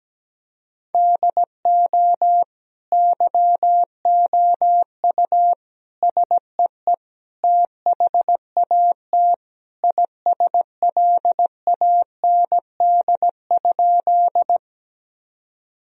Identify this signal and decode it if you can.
Morse code, heard as a keyed tone, about 17 wpm, 700 Hz. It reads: DO YOU SEE THAT ISLAND?
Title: Sentences from Top 1000 Words - Review Part 3 - 17wpm